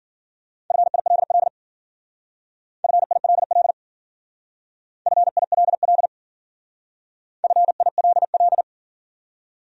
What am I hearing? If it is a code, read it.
Morse code, heard as a keyed tone, about 51 wpm, 700 Hz. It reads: FILL FILL FILL FILL